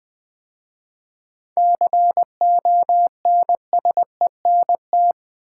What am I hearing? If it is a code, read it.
Morse code, heard as a keyed tone, about 20 wpm, 700 Hz. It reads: CONSENT
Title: 7 Letter Words - First 3 Letters Given - 20wpm